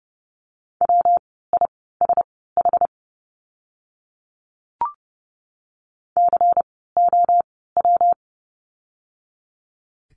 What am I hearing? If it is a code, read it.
Morse code, heard as a keyed tone, about 30 wpm, spaced as for 17 wpm, 700 Hz. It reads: WISH COW
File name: course / 30:10wpm / 115 Lesson 19f - Letter w - Words 30wpm